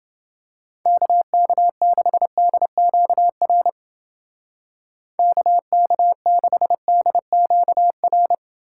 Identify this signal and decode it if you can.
Morse code, heard as a keyed tone, about 29 wpm, 700 Hz. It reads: KK6DQR KK6DQR